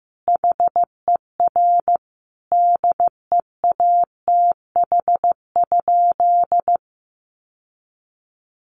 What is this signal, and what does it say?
Morse code, heard as a keyed tone, about 15 wpm, 700 Hz. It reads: HER DEATH?
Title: Sentences from Top 1500 Words 15wpm